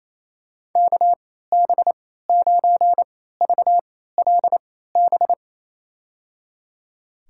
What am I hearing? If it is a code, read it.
Morse code, heard as a keyed tone, about 27 wpm, spaced as for 16 wpm, 700 Hz. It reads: KB9VLB
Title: USA Call Signs - 3x Character Spacing - 28wpm